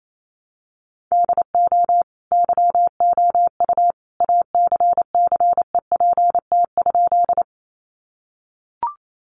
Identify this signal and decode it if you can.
Morse code, heard as a keyed tone, about 28 wpm, 700 Hz. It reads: DO YOU ACCEPT?
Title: Sentences from Top 1300 Words 28wpm